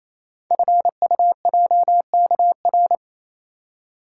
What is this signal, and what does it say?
Morse code, heard as a keyed tone, about 28 wpm, 700 Hz. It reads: FUJKR
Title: Code Groups - 5 Letters - 28wpm